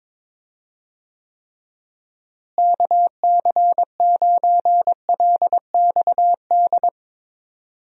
Morse code, heard as a keyed tone, about 21 wpm, 700 Hz. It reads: KC9LXD
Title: USA Call Signs - 6 Characters - Repeated 3x - 22wpm